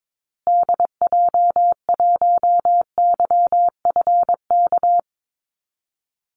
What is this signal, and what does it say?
Morse code, heard as a keyed tone, about 22 wpm, 700 Hz. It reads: DJ1YFK
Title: Call Signs Repeated 2x - 22wpm